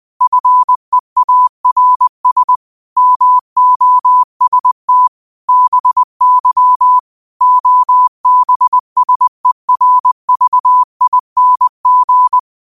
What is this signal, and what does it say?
Morse code, heard as a keyed tone, about 20 wpm, 1000 Hz. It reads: FEARS MOST BY OBSERVING
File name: qotdfiles / 20wpm